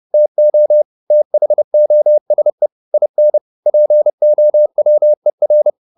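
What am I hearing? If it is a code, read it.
Morse code, heard as a keyed tone, about 30 wpm, 600 Hz. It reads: TO THOSE IN POWER
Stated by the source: Morse code at 30 wpm, the novel Never by Ken Follett in English, from CW med Gnister.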